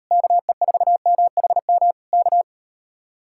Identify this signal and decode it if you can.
Morse code, heard as a keyed tone, about 38 wpm, 700 Hz. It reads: KE4MHM K